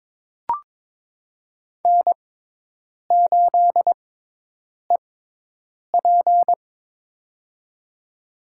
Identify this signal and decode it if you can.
Morse code, heard as a keyed tone, about 22 wpm, spaced as for 8 wpm, 700 Hz. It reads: N8EP
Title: USA Call Signs - 4 Characters - 6x Character Spacing - 22wpm